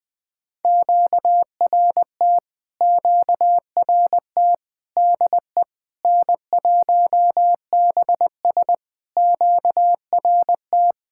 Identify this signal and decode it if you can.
Morse code, heard as a keyed tone, about 20 wpm, 700 Hz. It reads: QRT QRT DE N1BS QRT
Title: POTA QSO Practice - 20wpm